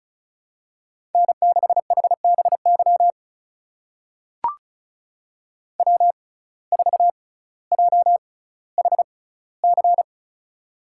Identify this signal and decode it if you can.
Morse code, heard as a keyed tone, about 34 wpm, spaced as for 18 wpm, 700 Hz. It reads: N6HBY W4JHC
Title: USA Call Signs - 5 Characters - 6x Character Spacing - 35wpm